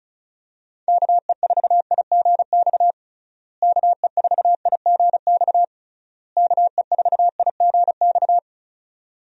Morse code, heard as a keyed tone, about 36 wpm, 700 Hz. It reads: KE4IGX KE4IGX KE4IGX
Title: USA Call Signs - 6 Characters - Repeated 3x - 35wpm